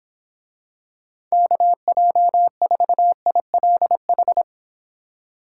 Morse code, heard as a keyed tone, about 26 wpm, 700 Hz. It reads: KJ4ILH